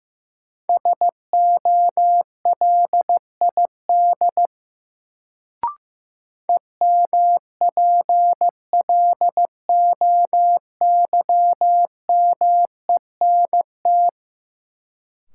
Morse code, heard as a keyed tone, about 15 wpm, 700 Hz. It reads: SOLID EMPLOYMENT
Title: Sentences from Top 1500 Words - New Word Drill Alt - 15wpm